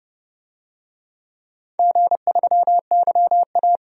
Morse code, heard as a keyed tone, about 30 wpm, 700 Hz. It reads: G3YA
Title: DXCC-100 Call Signs - 4 Characters - 30wpm